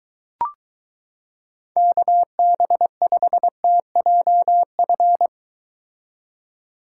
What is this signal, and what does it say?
Morse code, heard as a keyed tone, about 23 wpm, 700 Hz. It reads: KB5TJF